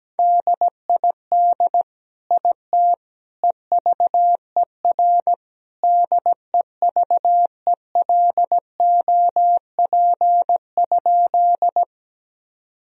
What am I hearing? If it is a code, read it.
Morse code, heard as a keyed tone, about 17 wpm, 700 Hz. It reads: DID IT EVER DEVELOP?
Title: Sentences from Top 1000 Words - Review Part 2 - 17wpm